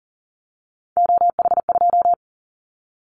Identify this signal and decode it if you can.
Morse code, heard as a keyed tone, about 42 wpm, 700 Hz. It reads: OH2
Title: Callsign Prefixes for Top 50 DXCC - 40wpm